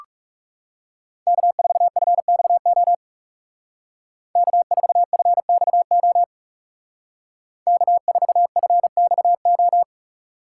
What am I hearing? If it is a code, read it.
Morse code, heard as a keyed tone, about 40 wpm, 700 Hz. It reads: K4FXO K4FXO K4FXO